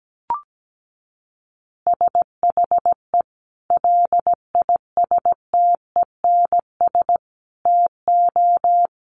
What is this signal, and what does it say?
Morse code, heard as a keyed tone, about 17 wpm, 700 Hz. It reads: SHE LISTENS TO
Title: Easy Phrases 17wpm